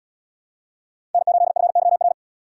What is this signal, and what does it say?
Morse code, heard as a keyed tone, about 77 wpm, 700 Hz. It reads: N8FYR